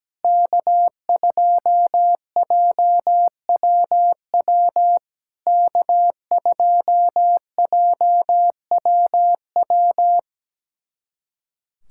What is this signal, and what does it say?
Morse code, heard as a keyed tone, about 17 wpm, 700 Hz. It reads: K2JWW K2JWW
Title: SOTA QSO Practice 17wpm